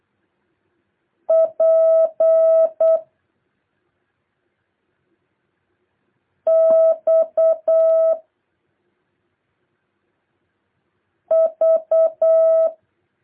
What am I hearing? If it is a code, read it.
Morse code, heard as a keyed tone, about 8 wpm, 650 Hz. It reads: P X V
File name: jh8cbh / 2023cw